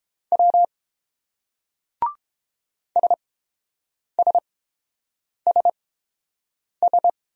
Morse code, heard as a keyed tone, about 28 wpm, spaced as for 7 wpm, 700 Hz. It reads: W SSSS